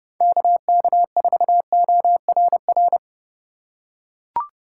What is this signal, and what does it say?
Morse code, heard as a keyed tone, about 30 wpm, 700 Hz. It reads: KK4ORR
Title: USA Call Signs - 6 Characters - 2x Character Spacing - 30wpm